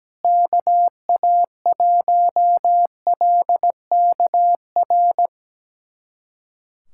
Morse code, heard as a keyed tone, about 17 wpm, 700 Hz. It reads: KA1LKR
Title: USA Callsign DE Callsign - 6 Characters - 17wpm